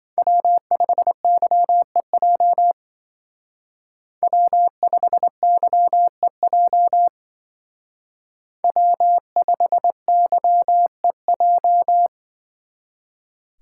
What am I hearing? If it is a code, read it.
Morse code, heard as a keyed tone, about 24 wpm, 700 Hz. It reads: W5YEJ W5YEJ W5YEJ
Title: Speed Racing - USA 1x3 Call Signs 20wpm